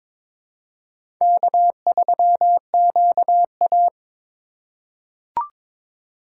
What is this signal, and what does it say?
Morse code, heard as a keyed tone, about 22 wpm, 700 Hz. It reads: K3QA
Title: USA Call Signs - 3x Character Spacing - 22wpm